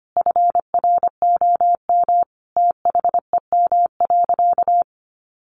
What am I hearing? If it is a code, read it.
Morse code, heard as a keyed tone, about 25 wpm, 700 Hz. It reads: FROM THEM.
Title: Sentences from Top 900 Words 25wpm